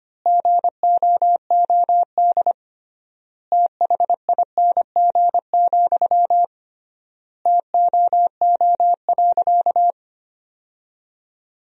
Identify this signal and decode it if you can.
Morse code, heard as a keyed tone, about 25 wpm, 700 Hz. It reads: GOOD THING, TOO.
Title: 3x Word-Spacing - Sentences from Top 200 Words - 25wpm